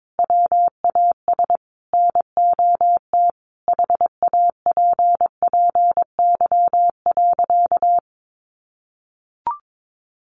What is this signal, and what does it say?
Morse code, heard as a keyed tone, about 22 wpm, 700 Hz. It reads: WAS NOT HAPPY.
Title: Sentences from Top 1400 Words 22wpm